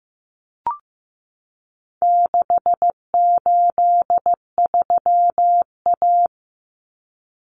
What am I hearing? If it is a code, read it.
Morse code, heard as a keyed tone, about 15 wpm, 700 Hz. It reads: 683A
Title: Common Cut Numbers - 4-Digit 15wpm